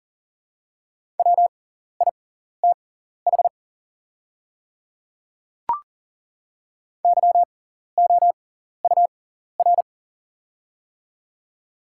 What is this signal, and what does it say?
Morse code, heard as a keyed tone, about 40 wpm, spaced as for 14 wpm, 700 Hz. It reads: WITH YOUR
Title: 6x Character Spacing - Top 100 Words - 40wpm